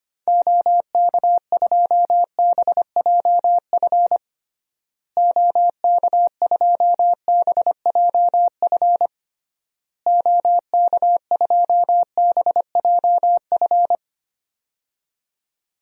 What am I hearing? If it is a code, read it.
Morse code, heard as a keyed tone, about 25 wpm, 700 Hz. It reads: OK2BJF OK2BJF OK2BJF
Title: DXCC-100 Call Signs - 6 Characters - Repeated 3x - 25wpm